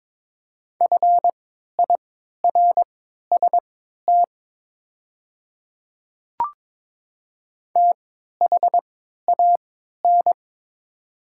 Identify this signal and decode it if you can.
Morse code, heard as a keyed tone, about 22 wpm, spaced as for 13 wpm, 700 Hz. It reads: FIRST THAN
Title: 3x Character Spacing - Top 100 Words - 22wpm